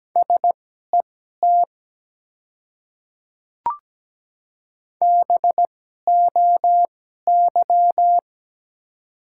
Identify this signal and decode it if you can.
Morse code, heard as a keyed tone, about 17 wpm, spaced as for 12 wpm, 700 Hz. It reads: SET BOY
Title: 2x Character Spacing - Top 200 Words - 17wpm